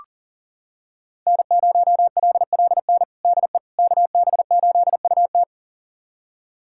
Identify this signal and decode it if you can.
Morse code, heard as a keyed tone, about 40 wpm, 700 Hz. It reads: N0LLN DE KB8UT
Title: USA Callsign DE Callsign - 5 Characters - 40wpm